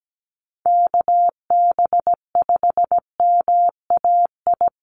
Morse code, heard as a keyed tone, about 17 wpm, 700 Hz. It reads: KB5MAI
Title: USA Call Signs 17wpm